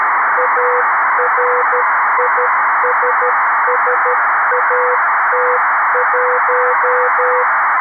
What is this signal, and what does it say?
Morse code, heard as a keyed tone, about 13 wpm, 500 Hz, with noise. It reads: ARISSAT1
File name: arissat / recordings / ARISSat 11Sept2011 1359utc 2mins 14frames